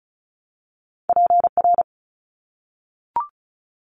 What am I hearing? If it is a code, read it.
Morse code, heard as a keyed tone, about 34 wpm, 700 Hz. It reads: PR
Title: ARRL Sections 35wpm